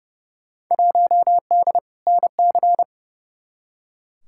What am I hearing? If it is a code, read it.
Morse code, heard as a keyed tone, about 30 wpm, 700 Hz. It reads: 1D NC
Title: Field Day Exchanges Encore - 30wpm